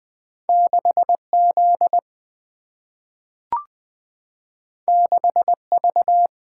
Morse code, heard as a keyed tone, about 20 wpm, 700 Hz. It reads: 6 Z 6 V
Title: Single-Letter Single-Number 20wpm